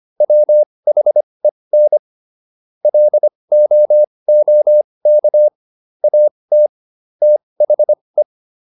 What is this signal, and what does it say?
Morse code, heard as a keyed tone, about 25 wpm, spaced as for 20 wpm, 600 Hz. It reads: WHEN LOOK AT THE
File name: WD405 20